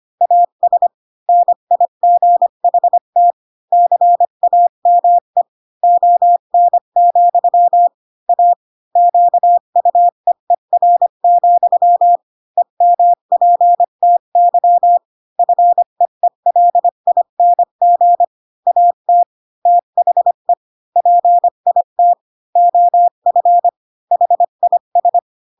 Morse code, heard as a keyed tone, about 25 wpm, 700 Hz. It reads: AS NIGHT CAME ON, A QUEER, EMPTY FEELING AT THE PIT OF HIS
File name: Pinocchio / Pinocchio ch5 20 25 ik1hge